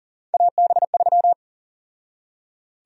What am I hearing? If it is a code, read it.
Morse code, heard as a keyed tone, about 40 wpm, 700 Hz. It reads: AB3
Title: USA Call Sign Prefixes - 40wpm